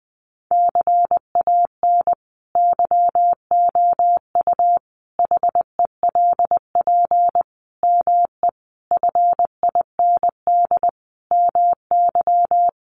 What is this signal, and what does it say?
Morse code, heard as a keyed tone, about 20 wpm, 700 Hz. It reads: CAN YOU HELP ME FIND MY
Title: Sentences from Top 200 Words 20wpm